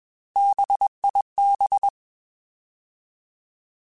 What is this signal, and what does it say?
Morse code, heard as a keyed tone, about 21 wpm, 800 Hz. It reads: BIB
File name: Alphabetical / Alphabetical mp3s / Bs @20